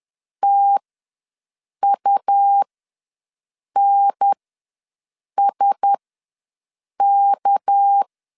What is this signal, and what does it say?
Morse code, heard as a keyed tone, about 10 wpm, spaced as for 6 wpm, 800 Hz. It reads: TUNSK
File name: LEZIONI / 7 GF9Z / 7 GF9Z ws 10wpm 3919